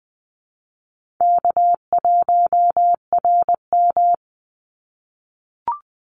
Morse code, heard as a keyed tone, about 20 wpm, 700 Hz. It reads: K1RM
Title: Easy Call Signs - 20wpm